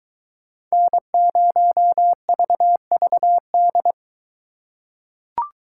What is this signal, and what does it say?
Morse code, heard as a keyed tone, about 22 wpm, 700 Hz. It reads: N0VVD